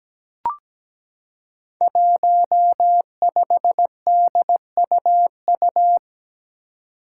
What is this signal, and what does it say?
Morse code, heard as a keyed tone, about 17 wpm, 700 Hz. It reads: 15DUU
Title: Five Challenging Letter-Number Combinations 17wpm